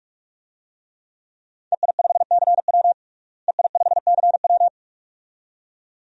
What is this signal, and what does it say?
Morse code, heard as a keyed tone, about 46 wpm, 700 Hz. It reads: EI5CW EI5CW
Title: DXCC-100 Call Signs - 5 Characters - Repeated 2x - 45wpm